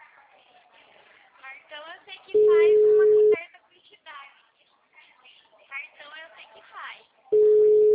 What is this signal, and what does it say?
Ringback tone (Europe, 425 Hz), with speech.